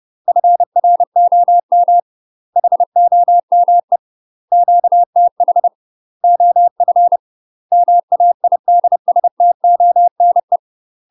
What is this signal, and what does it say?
Morse code, heard as a keyed tone, about 30 wpm, 700 Hz. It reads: FROM HOME QTH OF MAIDSTONE
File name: UK MASTER QSOs TRAINING/30WPM